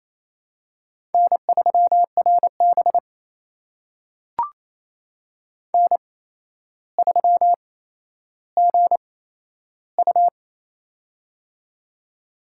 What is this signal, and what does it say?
Morse code, heard as a keyed tone, about 28 wpm, spaced as for 12 wpm, 700 Hz. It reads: N3RB N3GU
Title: USA Call Signs - 4 Characters - 8x Character Spacing - 28wpm